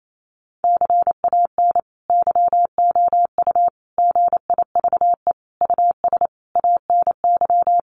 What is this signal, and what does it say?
Morse code, heard as a keyed tone, about 28 wpm, 700 Hz. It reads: CAN YOU GIVE US ANY
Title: Sentences from Top 1100 Words 28wpm